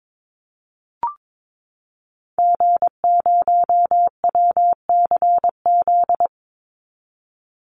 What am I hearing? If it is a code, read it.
Morse code, heard as a keyed tone, about 22 wpm, 700 Hz. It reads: G0WCZ
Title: Call Signs - 22wpm